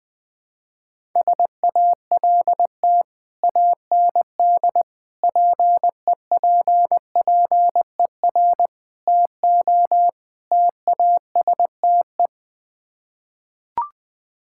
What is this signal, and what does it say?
Morse code, heard as a keyed tone, about 20 wpm, 700 Hz. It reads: SALT AND PEPPER TO TASTE
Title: Build A Phrase - 5-Word Phrases - 20wpm